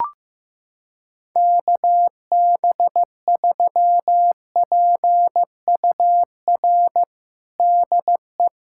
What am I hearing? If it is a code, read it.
Morse code, heard as a keyed tone, about 16 wpm, 700 Hz. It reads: KB3PUR DE
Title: USA Callsign DE Callsign - 6 Characters - 15wpm